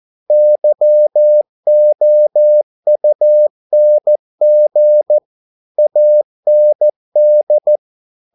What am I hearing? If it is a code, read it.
Morse code, heard as a keyed tone, about 14 wpm, 600 Hz. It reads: YOUNG AND